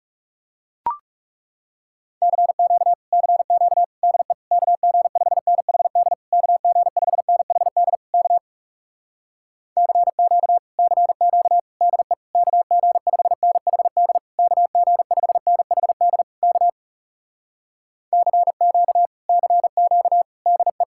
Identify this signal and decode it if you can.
Morse code, heard as a keyed tone, about 37 wpm, 700 Hz. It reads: CQ CQ DE KG5NHD KG5NHD K CQ CQ DE KG5NHD KG5NHD K CQ CQ DE
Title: Speed-Racing Common QSO Phrases - 30wpm